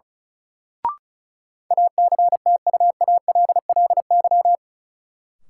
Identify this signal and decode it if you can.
Morse code, heard as a keyed tone, about 36 wpm, 700 Hz. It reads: ACTUALLY
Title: Sentences from Top 700 Words - New Word Drill - 35wpm